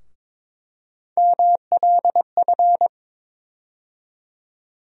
Morse code, heard as a keyed tone, about 22 wpm, 700 Hz. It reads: MLF
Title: USA Call Sign Postfixes - 3 Letters 22wpm